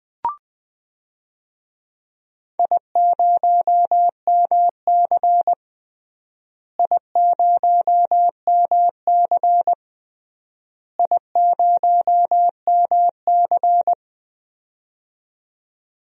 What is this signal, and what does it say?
Morse code, heard as a keyed tone, about 19 wpm, 700 Hz. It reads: I0MC I0MC I0MC
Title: DXCC-100 Call Signs - 4 Characters - Repeated 3x - 20wpm